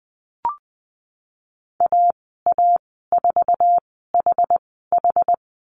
Morse code, heard as a keyed tone, about 20 wpm, spaced as for 15 wpm, 700 Hz. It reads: AA4HH
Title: 069 Lesson 12g - Letter d - Call Signs 20wpm